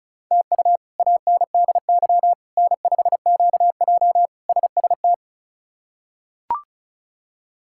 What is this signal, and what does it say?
Morse code, heard as a keyed tone, about 36 wpm, 700 Hz. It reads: TU ANDY N5QJ SST
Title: K1USN SST QSO Practice - 35wpm